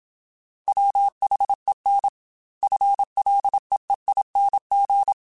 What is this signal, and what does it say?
Morse code, heard as a keyed tone, about 27 wpm, 800 Hz. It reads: WHEN FLEEING
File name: Chuckles / Chuckles-mp3s / Temptation @25